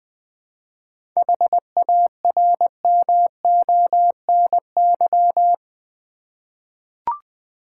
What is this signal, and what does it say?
Morse code, heard as a keyed tone, about 20 wpm, 700 Hz. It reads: HARMONY
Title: 7 Letter Words - First 3 Letters Given - 20wpm